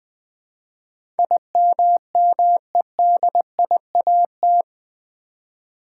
Morse code, heard as a keyed tone, about 20 wpm, 700 Hz. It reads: IMMEDIAT
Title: Build A Word - 4 Letters At A Time - 11-Letter Words - 20wpm